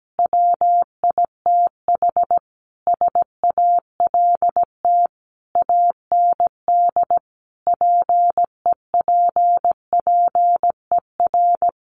Morse code, heard as a keyed tone, about 17 wpm, 700 Hz. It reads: WITH SALT AND PEPPER
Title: Sets of 5 Words 17wpm